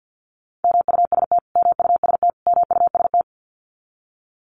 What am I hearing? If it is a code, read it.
Morse code, heard as a keyed tone, about 50 wpm, 700 Hz. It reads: MUST MUST MUST
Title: Top 500 Words Repeated 50wpm